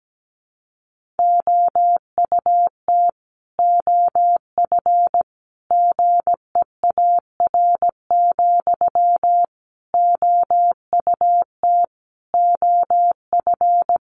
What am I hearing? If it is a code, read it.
Morse code, heard as a keyed tone, about 17 wpm, 700 Hz. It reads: OUT OF GEAR, OUT OF
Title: Sentences from Top 1400 Words 17wpm